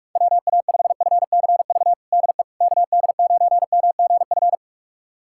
Morse code, heard as a keyed tone, about 46 wpm, 700 Hz. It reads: WA5FCV DE KD9MGF
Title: USA Callsign DE Callsign - 6 Characters - 45wpm